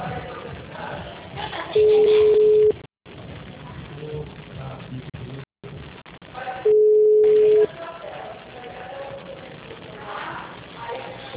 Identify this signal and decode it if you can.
Ringback tone (Europe, 425 Hz), with speech.
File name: out- -2008- - - .20668